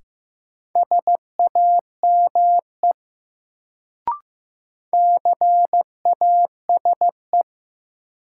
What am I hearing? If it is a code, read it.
Morse code, heard as a keyed tone, about 15 wpm, 700 Hz. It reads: SAME CASE
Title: Sets of 2 Words - New Word Drill - 15wpm